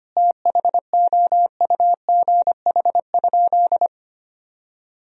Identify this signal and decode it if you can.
Morse code, heard as a keyed tone, about 25 wpm, 700 Hz. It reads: THOUGH?
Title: Sentences from Top 1000 Words - Review Part 2 - 25wpm